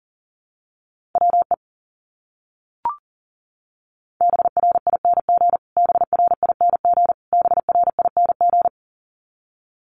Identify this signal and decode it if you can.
Morse code, heard as a keyed tone, about 37 wpm, 700 Hz. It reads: WE BRING BRING BRING
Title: Top 500 Words Repeated 40wpm